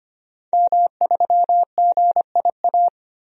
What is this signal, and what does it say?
Morse code, heard as a keyed tone, about 25 wpm, 700 Hz. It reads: M3GIA
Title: DXCC-100 Call Signs - 5 Characters - Repeated 2x - 25wpm